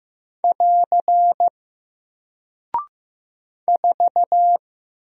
Morse code, heard as a keyed tone, about 15 wpm, 700 Hz. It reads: <AR> 4
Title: Single Character (Letter:Number:Punctuation:Prosign) Alt 15wpm